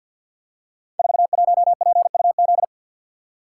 Morse code, heard as a keyed tone, about 50 wpm, 700 Hz. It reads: 41PUZ
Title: Code Groups - 5 Letter:Numbers - 50wpm